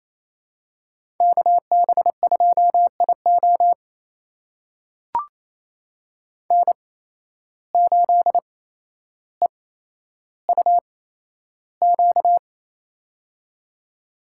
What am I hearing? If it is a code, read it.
Morse code, heard as a keyed tone, about 27 wpm, spaced as for 12 wpm, 700 Hz. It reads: KB2IO N8EUQ
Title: USA Call Signs - 5 Characters - 8x Character Spacing - 28wpm